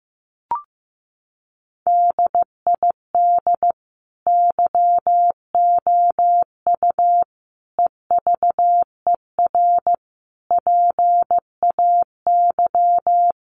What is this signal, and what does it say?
Morse code, heard as a keyed tone, about 14 wpm, 700 Hz. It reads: DID YOU EVER PAY
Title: Sentences from Top 700 Words 15wpm